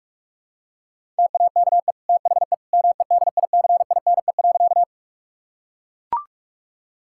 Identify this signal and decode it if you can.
Morse code, heard as a keyed tone, about 41 wpm, 700 Hz. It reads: TAKE THE MEDICINE.
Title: Sentences from Top 1000 Words - Review Part 6 - 45wpm